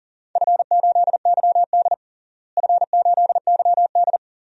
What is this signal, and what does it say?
Morse code, heard as a keyed tone, about 37 wpm, 690 Hz. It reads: F8YD F8YD
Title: DXCC-100 Call Signs - 4 Characters - Repeated 2x - 40wpm